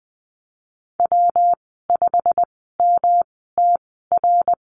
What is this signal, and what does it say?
Morse code, heard as a keyed tone, about 20 wpm, spaced as for 15 wpm, 700 Hz. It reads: W5MTR
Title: 161 Lesson 26g - Letter g - Call Signs 20wpm